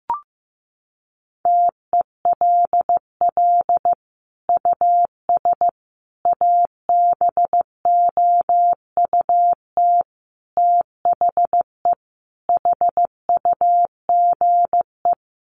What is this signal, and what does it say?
Morse code, heard as a keyed tone, about 14 wpm, 700 Hz. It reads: TELL US ABOUT THE HUGE
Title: Sentences from Top 1100 Words 15wpm